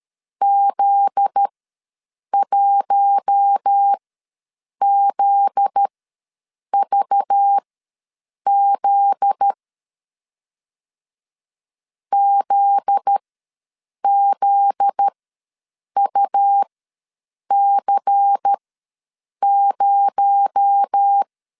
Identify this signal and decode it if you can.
Morse code, heard as a keyed tone, about 13 wpm, spaced as for 7 wpm, 800 Hz. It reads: Z1ZVZ ZZUC0
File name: 7 GF9Z ws 12wpm